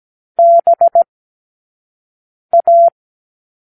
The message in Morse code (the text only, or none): BA